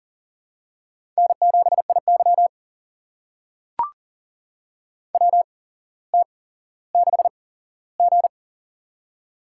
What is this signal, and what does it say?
Morse code, heard as a keyed tone, about 38 wpm, spaced as for 17 wpm, 700 Hz. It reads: N7IY WT6G
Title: USA Call Signs - 8x Character Spacing - 40wpm